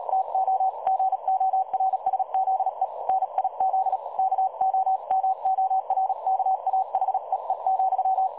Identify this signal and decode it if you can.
Morse code, heard as a keyed tone, about 38 wpm, 760 Hz. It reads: TO GORSZE NIZ KOMORKA HEE,
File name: dlazaawansowanych / nagraniazpasma / websdr recording 3565.0kHz